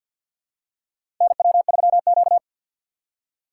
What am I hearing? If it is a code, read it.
Morse code, heard as a keyed tone, about 52 wpm, 700 Hz. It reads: NW3Q
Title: USA Call Signs - 4 Characters - 2x Character Spacing - 50wpm